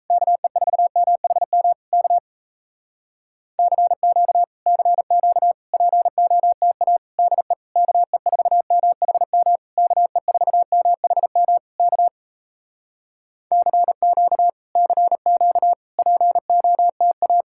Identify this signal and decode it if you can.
Morse code, heard as a keyed tone, about 37 wpm, 700 Hz. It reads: KE4MHM K CQ CQ POTA DE KE4MHM KE4MHM K CQ CQ POTA